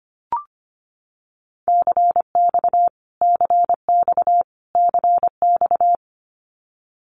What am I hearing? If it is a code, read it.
Morse code, heard as a keyed tone, about 25 wpm, 700 Hz. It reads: CX CX CX
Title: Abbreviations Repeated 3x - 25wpm